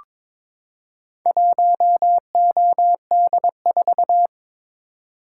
Morse code, heard as a keyed tone, about 22 wpm, 700 Hz. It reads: 1OD4
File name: practice / 22wpm / Four Letter-Number Combinations - 22wpm